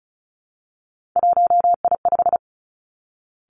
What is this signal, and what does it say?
Morse code, heard as a keyed tone, about 36 wpm, 700 Hz. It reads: 1I5
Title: Three Challenging Letter-Number Combinations 35wpm